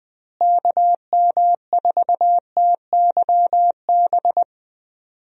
Morse code, heard as a keyed tone, about 21 wpm, 700 Hz. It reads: KM4TYB
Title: USA Call Signs - 6 Characters - Repeated 2x - 20wpm